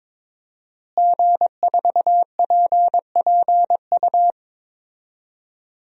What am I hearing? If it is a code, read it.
Morse code, heard as a keyed tone, about 22 wpm, 700 Hz. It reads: G4PPU
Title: DXCC-100 Call Signs - 5 Characters - Repeated 2x - 22wpm